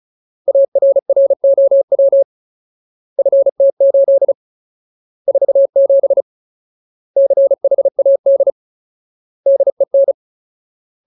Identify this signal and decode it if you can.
Morse code, heard as a keyed tone, about 34 wpm, 550 Hz. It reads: ARROW FT8 47 CHAD DEN
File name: Ultimate Word List / 35WPM Ultimate